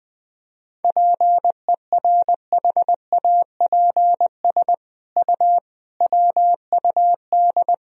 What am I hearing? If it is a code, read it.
Morse code, heard as a keyed tone, about 20 wpm, 700 Hz. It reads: PERHAPS U WUD
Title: Sentences from Top 500 Words and Abbreviations 20wpm